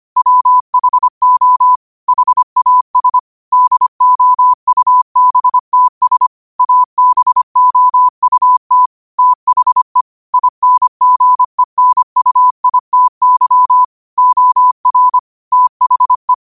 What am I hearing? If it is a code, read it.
Morse code, heard as a keyed tone, about 25 wpm, 1000 Hz. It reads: WHO HAS DOUBTS ABOUT THE INGENUITY OR THE